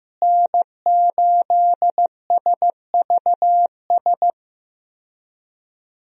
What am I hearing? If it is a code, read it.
Morse code, heard as a keyed tone, about 15 wpm, 700 Hz. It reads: N8SVS
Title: USA Call Signs - 5 Characters - Repeated 2x - 15wpm